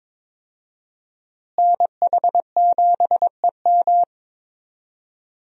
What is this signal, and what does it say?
Morse code, heard as a keyed tone, about 22 wpm, 700 Hz. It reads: NH7EM
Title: USA Call Signs - 5 Characters - Repeated 3x - 22wpm